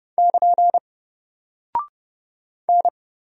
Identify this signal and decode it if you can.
Morse code, heard as a keyed tone, about 30 wpm, 700 Hz. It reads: <KN> N
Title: Single Character (Letter:Number:Punctuation:Prosign) Alt 30wpm